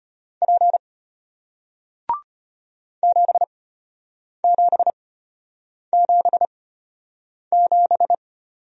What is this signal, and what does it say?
Morse code, heard as a keyed tone, about 33 wpm, spaced as for 8 wpm, 700 Hz. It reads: P 7777